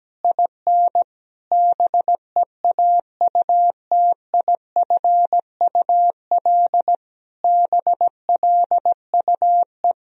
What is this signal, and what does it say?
Morse code, heard as a keyed tone, about 17 wpm, 700 Hz. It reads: IN BEAUTIFUL BLUE
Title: Sentences from Top 1000 Words - Review Part 3 - 17wpm